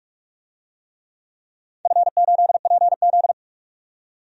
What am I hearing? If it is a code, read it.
Morse code, heard as a keyed tone, about 43 wpm, 700 Hz. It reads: F8PZ
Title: DXCC-100 Call Signs - 4 Characters - 45wpm